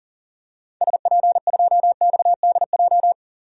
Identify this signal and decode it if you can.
Morse code, heard as a keyed tone, about 37 wpm, 700 Hz. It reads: SP2XDJ
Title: DXCC-100 Call Signs - 6 Characters - 40wpm